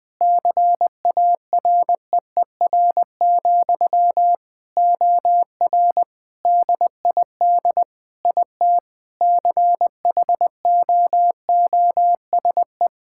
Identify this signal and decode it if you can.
Morse code, heard as a keyed tone, about 19 wpm, 700 Hz. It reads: CAREER, OR DID IT CHOOSE
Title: Sentences from Top 1000 Words - Review Part 6 - 20wpm